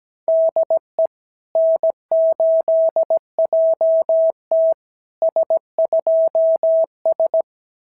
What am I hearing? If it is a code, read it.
Morse code, heard as a keyed tone, about 17 wpm, 650 Hz. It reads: DE N8JT S2S
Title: SOTA QSO Practice 17wpm